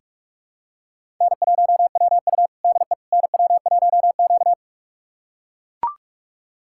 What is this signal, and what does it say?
Morse code, heard as a keyed tone, about 41 wpm, 700 Hz. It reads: N1WU DE NW1Q
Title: USA Callsign DE Callsign - 4 Characters - 45wpm